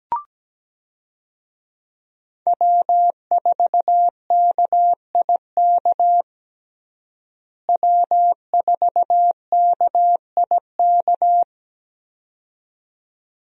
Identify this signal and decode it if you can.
Morse code, heard as a keyed tone, about 17 wpm, 700 Hz. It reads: W4KIK W4KIK
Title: USA Call Signs - 5 Characters - Repeated 2x - 17wpm